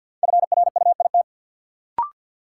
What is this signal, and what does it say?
Morse code, heard as a keyed tone, about 52 wpm, 700 Hz. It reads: FRUIT
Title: Sentences from Top 1300 Words - New Word Drill Alt - 50wpm